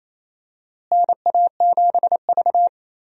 Morse code, heard as a keyed tone, about 28 wpm, 700 Hz. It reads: NA7V
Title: USA Call Signs - 4 Characters - Repeated 2x - 28wpm